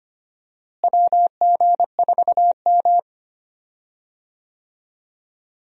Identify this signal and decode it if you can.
Morse code, heard as a keyed tone, about 25 wpm, 700 Hz. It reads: WG4M
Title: USA Call Signs - 4 Characters - Repeated 2x - 25wpm